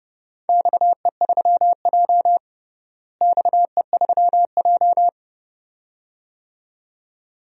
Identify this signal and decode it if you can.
Morse code, heard as a keyed tone, about 31 wpm, 700 Hz. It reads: XE3J XE3J
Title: DXCC-100 Call Signs - 4 Characters - Repeated 2x - 30wpm